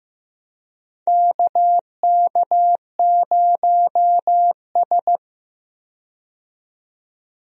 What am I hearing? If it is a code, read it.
Morse code, heard as a keyed tone, about 15 wpm, 700 Hz. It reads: KK0S
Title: USA Call Signs - 4 Characters - 15wpm